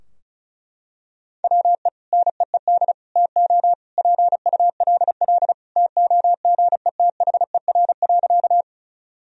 Morse code, heard as a keyed tone, about 34 wpm, 700 Hz. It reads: WE NEED TO PULL TOGETHER.
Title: Sentences from Top 1000 Words - Review Part 2 - 35wpm